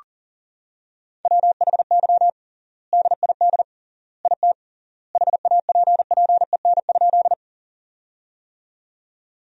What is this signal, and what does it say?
Morse code, heard as a keyed tone, about 40 wpm, 700 Hz. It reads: WHY DID IT HAPPEN?
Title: 3x Word-Spacing - Sentences from Top 400 Words - 40wpm